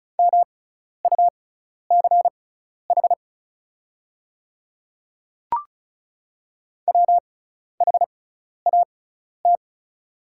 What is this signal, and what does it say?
Morse code, heard as a keyed tone, about 36 wpm, spaced as for 12 wpm, 700 Hz. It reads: MUCH WHAT
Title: 6x Character Spacing - Top 300 Words - 35wpm